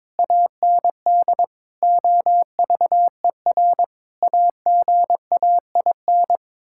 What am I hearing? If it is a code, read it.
Morse code, heard as a keyed tone, about 21 wpm, 700 Hz. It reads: AND OVER AGAIN
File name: practice / 22wpm / Sets of 8 Words - 22wpm